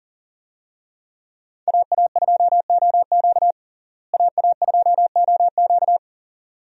Morse code, heard as a keyed tone, about 37 wpm, 700 Hz. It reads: AA2OQ AA2OQ
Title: USA Call Signs - 5 Characters - Repeated 2x - 40wpm